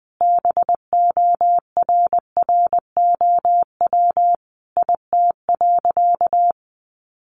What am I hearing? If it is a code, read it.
Morse code, heard as a keyed tone, about 20 wpm, 700 Hz. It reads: BORROW IT.
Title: Sentences from Top 1500 Words 20wpm